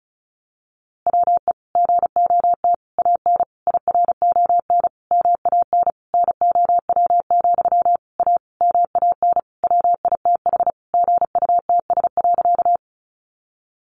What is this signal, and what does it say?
Morse code, heard as a keyed tone, about 34 wpm, 700 Hz. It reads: WE GOT AN IRON MAN NOW, A MAN WITH GUTS.
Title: A Sound Of Thunder 35wpm